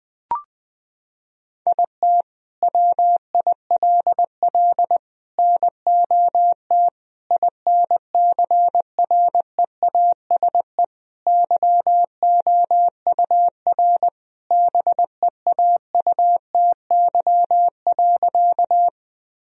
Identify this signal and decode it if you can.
Morse code, heard as a keyed tone, about 20 wpm, 700 Hz. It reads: IT WILL NOT INCREASE YOUR BEAUTY.
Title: Sentences from Top 1000 Words - Review Part 4 - 20wpm